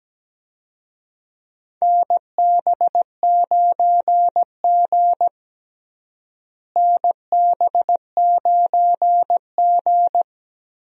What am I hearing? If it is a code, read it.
Morse code, heard as a keyed tone, about 17 wpm, 700 Hz. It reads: NB9G NB9G
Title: USA Call Signs - 4 Characters - Repeated 3x - 17wpm